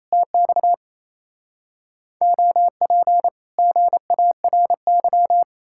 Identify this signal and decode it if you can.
Morse code, heard as a keyed tone, about 38 wpm, 700 Hz. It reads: TX OP GARY